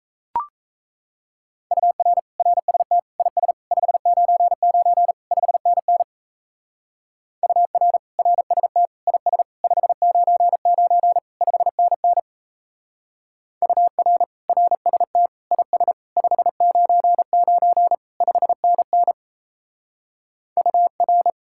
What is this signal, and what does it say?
Morse code, heard as a keyed tone, about 33 wpm, 700 Hz. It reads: UR RST IS 599 5NN UR RST IS 599 5NN UR RST IS 599 5NN UR
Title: Speed-Racing Common QSO Phrases - 28wpm